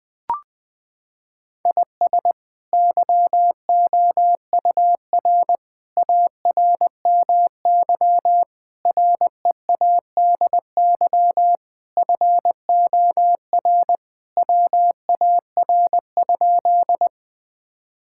Morse code, heard as a keyed tone, about 20 wpm, 700 Hz. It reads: IS YOUR ARMY READY FOR WAR?
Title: Sentences from Top 1000 Words - Review Part 5 - 20wpm